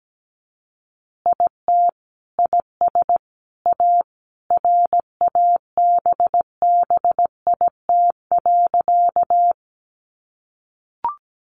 Morse code, heard as a keyed tone, about 17 wpm, 700 Hz. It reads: IT IS A RABBIT.
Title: Easy Phrases 17wpm